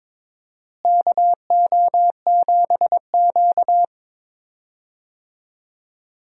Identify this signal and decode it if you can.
Morse code, heard as a keyed tone, about 22 wpm, 700 Hz. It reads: KO7Q
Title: USA Call Signs - 4 Characters - Repeated 2x - 22wpm